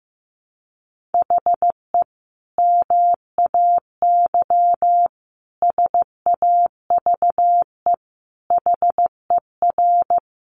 Morse code, heard as a keyed tone, about 15 wpm, 700 Hz. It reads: HE MAY SAVE HER
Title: Sentences from Top 1100 Words 15wpm